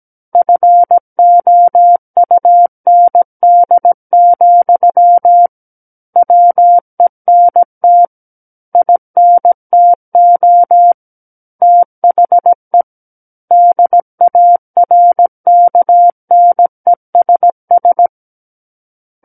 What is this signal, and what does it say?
Morse code, heard as a keyed tone, about 17 wpm, 700 Hz. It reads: FOUND, WENT INTO THE DARKNESS
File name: Book One / 08-Chapter 8 - 17 WPM